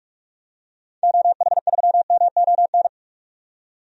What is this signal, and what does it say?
Morse code, heard as a keyed tone, about 44 wpm, 690 Hz. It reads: OH3MON
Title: DXCC-100 Call Signs - 6 Characters - 45wpm